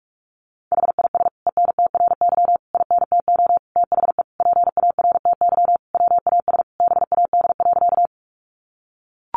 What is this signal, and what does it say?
Morse code, heard as a keyed tone, about 44 wpm, 700 Hz. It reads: HIS ENTRY INTO THE PARTY WAS BAD.
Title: Sentences from Top 1500 Words 45wpm